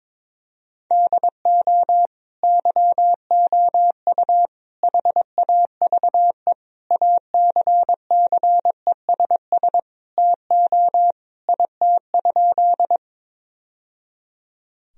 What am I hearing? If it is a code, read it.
Morse code, heard as a keyed tone, about 22 wpm, 700 Hz. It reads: DO YOU HAVE ACCESS TO IT?
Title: Sentences from Top 1000 Words - Review Part 4 - 22wpm